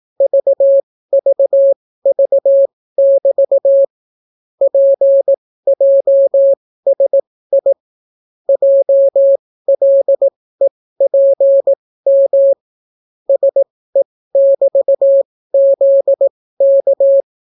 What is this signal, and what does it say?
Morse code, heard as a keyed tone, about 18 wpm, spaced as for 14 wpm, 550 Hz. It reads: VVV<BT> PJSI JLEPM SE<BT>ZK